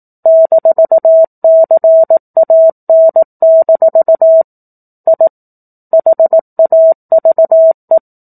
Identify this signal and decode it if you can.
Morse code, heard as a keyed tone, about 18 wpm, 650 Hz. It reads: -CAN- I HAVE
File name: 04-Chapter-04 18 WPM